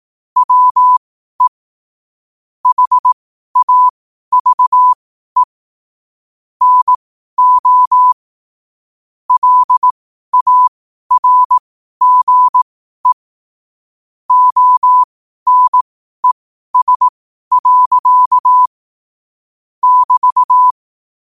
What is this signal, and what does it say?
Morse code, heard as a keyed tone, about 18 wpm, spaced as for 12 wpm, 1000 Hz. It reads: WE HAVE NO LARGE ONES. <BT>
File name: qotdfiles / 12wpm